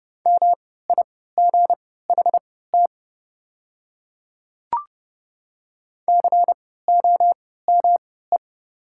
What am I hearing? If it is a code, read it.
Morse code, heard as a keyed tone, about 30 wpm, spaced as for 17 wpm, 700 Hz. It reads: MIGHT COME
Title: 3x Character Spacing - Top 500 Words - 30wpm